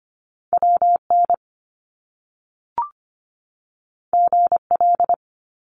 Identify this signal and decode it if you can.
Morse code, heard as a keyed tone, about 25 wpm, 700 Hz. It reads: W N G L
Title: Top Two Letter Combinations in Words 25wpm